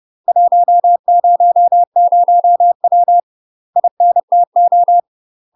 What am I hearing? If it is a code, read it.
Morse code, heard as a keyed tone, about 30 wpm, 700 Hz. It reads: 100W INTO
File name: UK MASTER QSOs TRAINING / 30WPM / UK MASTER QSOs 0009